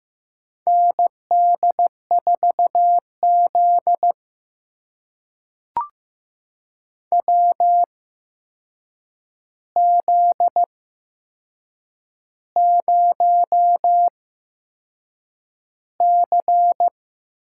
Morse code, heard as a keyed tone, about 15 wpm, spaced as for 6 wpm, 700 Hz. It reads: ND4Z WZ0C
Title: USA Call Signs - 4 Characters - 8x Character Spacing - 15wpm